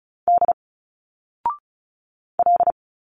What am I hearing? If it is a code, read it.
Morse code, heard as a keyed tone, about 36 wpm, 700 Hz. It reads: D L
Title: Single Letters 35wpm